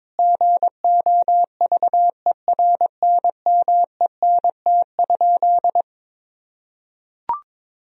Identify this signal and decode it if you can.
Morse code, heard as a keyed tone, about 21 wpm, 700 Hz. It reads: GOVERNMENT?
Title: Sentences from Top 1000 Words - Review Part 3 - 22wpm